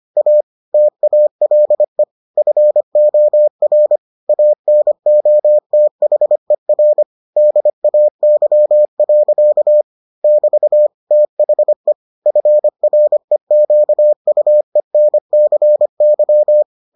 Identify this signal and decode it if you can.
Morse code, heard as a keyed tone, about 25 wpm, 600 Hz. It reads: A TALE FOR ANOTHER DAY. <BT> THE FREQUENCY